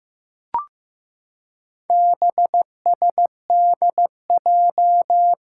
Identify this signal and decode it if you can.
Morse code, heard as a keyed tone, about 15 wpm, 700 Hz. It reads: BSDJ
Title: Four Challenging Letter-Number Combinations 15wpm